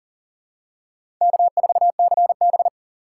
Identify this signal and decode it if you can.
Morse code, heard as a keyed tone, about 40 wpm, 700 Hz. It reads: K4CB
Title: USA Call Signs - 4 Characters - 4x Character Spacing - 40wpm